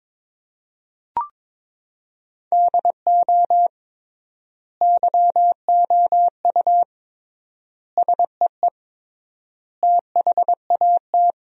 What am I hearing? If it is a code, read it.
Morse code, heard as a keyed tone, about 22 wpm, 700 Hz. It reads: DO YOU SEE THAT
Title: 3x Word-Spacing - Sentences from Top 500 Words - 22wpm